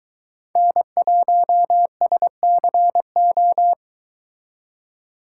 Morse code, heard as a keyed tone, about 22 wpm, 700 Hz. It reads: N1SCO